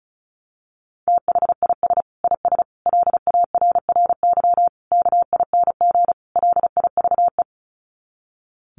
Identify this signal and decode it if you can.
Morse code, heard as a keyed tone, about 34 wpm, 700 Hz. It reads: THIS IS LARRY KING LIVE
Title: Sets of 5 Words Encore2 35wpm